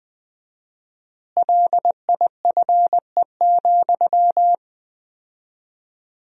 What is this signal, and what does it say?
Morse code, heard as a keyed tone, about 21 wpm, 700 Hz. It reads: LIFE,
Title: 4x Word-Spacing - Sentences from Top 300 Words - 20wpm